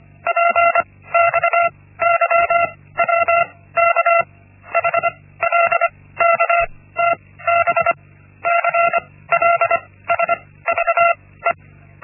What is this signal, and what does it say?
Morse code, heard as a keyed tone, about 25 wpm, spaced as for 16 wpm, 680 Hz. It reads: PXYWK HLKTB CLSVE
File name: Linksmakalnis-1